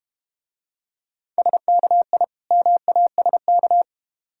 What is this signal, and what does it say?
Morse code, heard as a keyed tone, about 31 wpm, 700 Hz. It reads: SKI MASK
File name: practice / 25wpm / Difficult Words - Speed-Racing with Hints - 25wpm